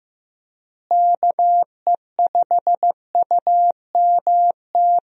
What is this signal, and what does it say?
Morse code, heard as a keyed tone, about 15 wpm, 700 Hz. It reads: KE5UMT
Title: USA Call Signs - 6 Characters - Repeated 2x - 15wpm